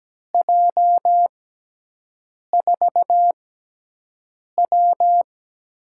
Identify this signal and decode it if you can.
Morse code, heard as a keyed tone, about 17 wpm, spaced as for 6 wpm, 700 Hz. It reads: J4W
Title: USA Call Signs - 6x Character Spacing - 17wpm